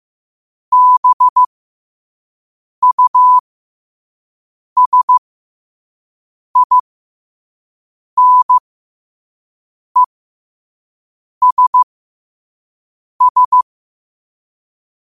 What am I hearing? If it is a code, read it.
Morse code, heard as a keyed tone, about 15 wpm, spaced as for 5 wpm, 1000 Hz. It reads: BUSINESS
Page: Quotes for Sat, 16 Aug 2025 in Morse Code at 5 words per minute.